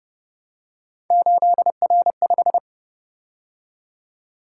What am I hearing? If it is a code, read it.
Morse code, heard as a keyed tone, about 31 wpm, 700 Hz. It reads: 8R5